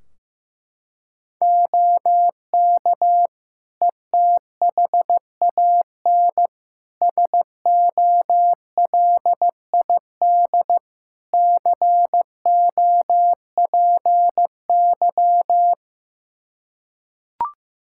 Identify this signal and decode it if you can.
Morse code, heard as a keyed tone, about 15 wpm, 700 Hz. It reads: OK ETHAN SOLID COPY
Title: Practice QSO - Exchange 3 - More Info - 15wpm